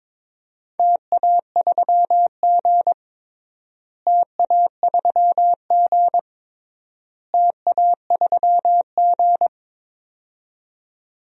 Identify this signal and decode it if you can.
Morse code, heard as a keyed tone, about 22 wpm, 700 Hz. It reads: TA3G TA3G TA3G
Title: DXCC-100 Call Signs - 4 Characters - Repeated 3x - 22wpm